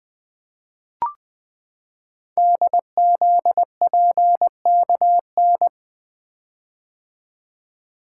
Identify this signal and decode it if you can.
Morse code, heard as a keyed tone, about 20 wpm, 700 Hz. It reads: DZPKN
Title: Code Groups - 5 Letters - 20wpm